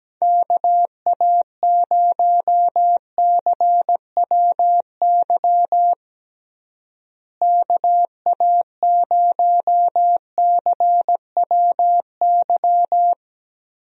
Morse code, heard as a keyed tone, about 17 wpm, 700 Hz. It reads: KA0CWY KA0CWY
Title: USA Call Signs - 6 Characters - Repeated 3x - 17wpm